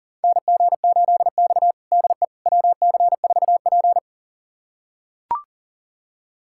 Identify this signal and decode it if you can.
Morse code, heard as a keyed tone, about 40 wpm, 700 Hz. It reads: NG8X DE WC4P
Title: USA Callsign DE Callsign - 4 Characters - 40wpm